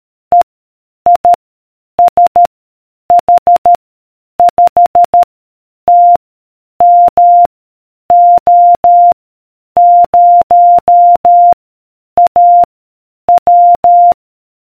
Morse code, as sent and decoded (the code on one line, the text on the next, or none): . / .. / ... / .... / ..... / - / -- / --- / ----- / .- / .--
E I S H 5 T M O 0 A W